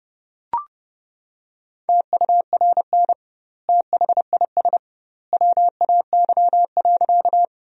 Morse code, heard as a keyed tone, about 30 wpm, 700 Hz. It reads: TURN THIS WAY.
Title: 2x Word-Spacing - Sentences from Top 200 Words - 30wpm